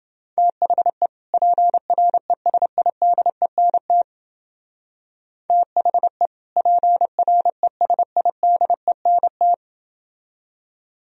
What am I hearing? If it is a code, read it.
Morse code, heard as a keyed tone, about 28 wpm, 700 Hz. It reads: THE PRESIDENT THE PRESIDENT